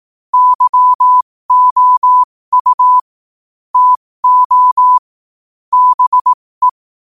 Morse code, heard as a keyed tone, about 18 wpm, spaced as for 15 wpm, 1000 Hz. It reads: YOU TO BE